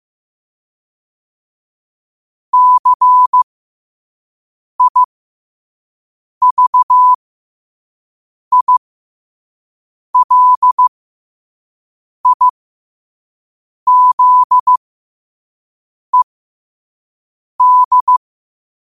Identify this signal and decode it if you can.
Morse code, heard as a keyed tone, about 15 wpm, spaced as for 5 wpm, 1000 Hz. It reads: CIVILIZED
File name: qotdfiles / 5wpm